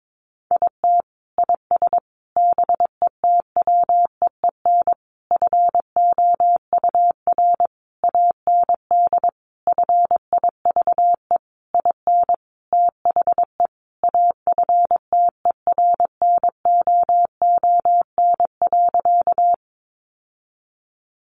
Morse code, heard as a keyed tone, about 22 wpm, 700 Hz. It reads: IT IS BETWEEN FOUR AND FIVE IN THE AFTERNOON.
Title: Sentences from Top 1500 Words 22wpm